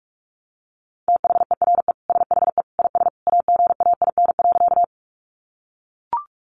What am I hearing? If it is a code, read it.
Morse code, heard as a keyed tone, about 44 wpm, 700 Hz. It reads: THERE SHE IS AGAIN.
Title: Sentences from Top 200 Words 45wpm